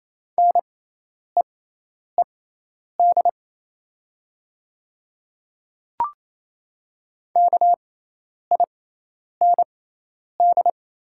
Morse code, heard as a keyed tone, about 28 wpm, spaced as for 10 wpm, 700 Hz. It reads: NEED KIND
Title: 6x Character Spacing - Top 200 Words - 28wpm